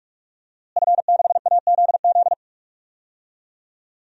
Morse code, heard as a keyed tone, about 45 wpm, 700 Hz. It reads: F6AZZ
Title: DXCC-100 Call Signs - 5 Characters - Repeated 3x - 45wpm